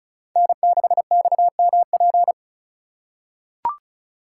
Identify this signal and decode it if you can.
Morse code, heard as a keyed tone, about 36 wpm, 690 Hz. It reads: N6XMP
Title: USA Call Signs - 3x Character Spacing - 35wpm